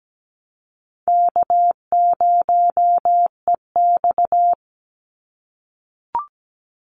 Morse code, heard as a keyed tone, about 17 wpm, 700 Hz. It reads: K0EX
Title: Call Signs - 17wpm